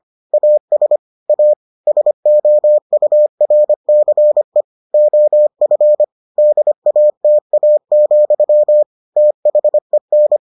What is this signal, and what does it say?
Morse code, heard as a keyed tone, about 25 wpm, 600 Hz. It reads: AS A SOURCE OF DATA, THEN